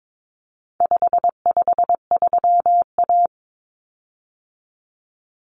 Morse code, heard as a keyed tone, about 22 wpm, 700 Hz. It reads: 553A
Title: Common Cut Numbers - 4-Digit 22wpm